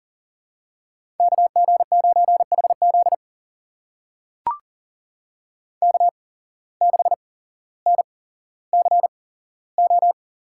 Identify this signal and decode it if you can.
Morse code, heard as a keyed tone, about 40 wpm, spaced as for 17 wpm, 700 Hz. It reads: KG9HZ K6NCO
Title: USA Call Signs - 5 Characters - 8x Character Spacing - 40wpm